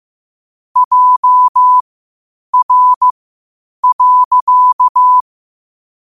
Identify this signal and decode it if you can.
Morse code, heard as a keyed tone, about 15 wpm, spaced as for 8 wpm, 1000 Hz. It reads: JR.
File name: qotdfiles / 8wpm